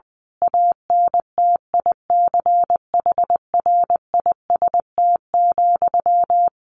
Morse code, heard as a keyed tone, about 20 wpm, 700 Hz. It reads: ANTICHRIST,
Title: A Sound Of Thunder 20wpm